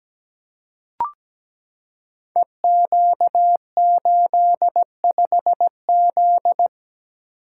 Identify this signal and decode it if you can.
Morse code, heard as a keyed tone, about 17 wpm, 700 Hz. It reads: EQ85Z
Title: Code Groups - 5 Letter:Numbers - 17wpm